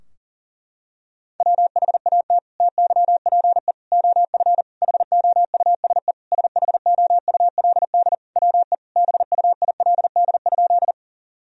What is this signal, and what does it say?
Morse code, heard as a keyed tone, about 37 wpm, 700 Hz. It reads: WHAT TYPE OF HOUSE SHOULD WE BUILD?
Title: Sentences from Top 1000 Words - Review Part 3 - 40wpm